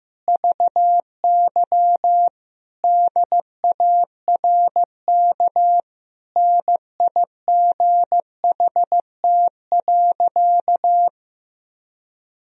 Morse code, heard as a keyed tone, about 14 wpm, 700 Hz. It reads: VY DARK NIGHT.
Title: Sentences from Top 500 Words and Abbreviations 15wpm